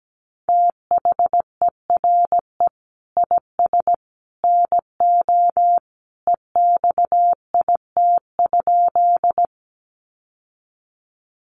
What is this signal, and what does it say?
Morse code, heard as a keyed tone, about 17 wpm, 700 Hz. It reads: THERE IS NO EXIT?
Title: Sentences from Top 1100 Words 17wpm